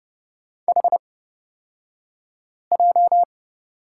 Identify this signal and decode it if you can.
Morse code, heard as a keyed tone, about 30 wpm, spaced as for 5 wpm, 700 Hz. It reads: HJ
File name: Lesson 37d - Letter x - Two Characters 30wpm